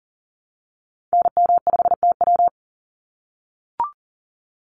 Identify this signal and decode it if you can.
Morse code, heard as a keyed tone, about 40 wpm, 700 Hz. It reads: NM5TW
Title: NA SOTA Activator Callsigns 40wpm